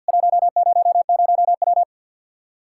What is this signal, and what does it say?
Morse code, heard as a keyed tone, about 52 wpm, 700 Hz. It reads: 100W
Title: Practice QSO - Exchange 2 - Info - 50wpm